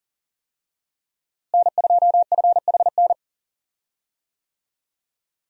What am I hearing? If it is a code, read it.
Morse code, heard as a keyed tone, about 40 wpm, 700 Hz. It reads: N2FHN
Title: USA Call Signs - 5 Characters - Repeated 2x - 40wpm